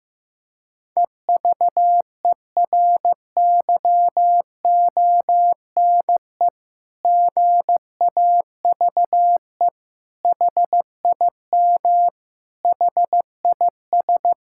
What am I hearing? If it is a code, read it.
Morse code, heard as a keyed tone, about 15 wpm, 700 Hz. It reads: EVERYONE GAVE HIM HIS
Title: Sentences from Top 1000 Words - Review Part 6 - 15wpm